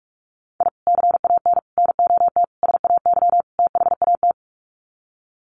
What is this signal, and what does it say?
Morse code, heard as a keyed tone, about 44 wpm, 700 Hz. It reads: I CAN NOT SAY THAT
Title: Sets of 5 Words Encore2 45wpm